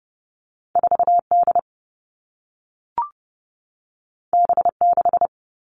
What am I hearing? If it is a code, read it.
Morse code, heard as a keyed tone, about 30 wpm, 700 Hz. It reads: 4D B6
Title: All Cut Numbers - 2-Digit 30wpm